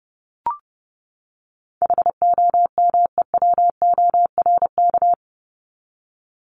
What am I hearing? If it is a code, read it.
Morse code, heard as a keyed tone, about 30 wpm, 700 Hz. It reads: HOMEWORK
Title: Top 1501-1750 Words 30wpm